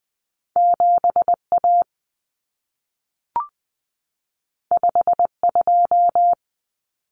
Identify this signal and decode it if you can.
Morse code, heard as a keyed tone, about 20 wpm, 700 Hz. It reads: 7A 52
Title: Common Cut Numbers - 2-Digit 20wpm